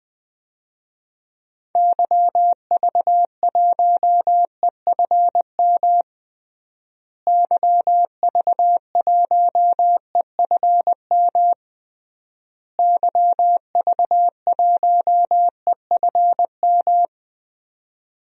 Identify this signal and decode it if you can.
Morse code, heard as a keyed tone, about 20 wpm, 700 Hz. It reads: YV1EFM YV1EFM YV1EFM
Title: DXCC-100 Call Signs - 6 Characters - Repeated 3x - 20wpm